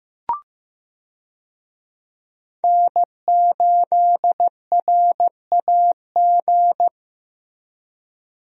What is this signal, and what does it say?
Morse code, heard as a keyed tone, about 16 wpm, 700 Hz. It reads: N8RAG
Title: USA Call Signs - 5 Characters - Repeated 2x - 15wpm